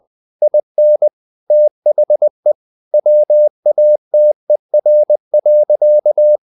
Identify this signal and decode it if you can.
Morse code, heard as a keyed tone, about 20 wpm, 600 Hz. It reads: IN THE WATER.
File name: the old man and the sea part 2 20wpm